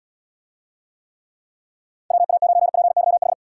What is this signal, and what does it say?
Morse code, heard as a keyed tone, about 78 wpm, 700 Hz. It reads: WA9PYH